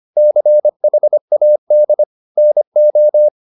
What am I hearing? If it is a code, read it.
Morse code, heard as a keyed tone, about 25 wpm, 600 Hz. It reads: CHAD NO